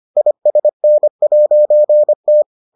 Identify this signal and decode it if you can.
Morse code, heard as a keyed tone, about 25 wpm, 600 Hz. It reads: ISN'T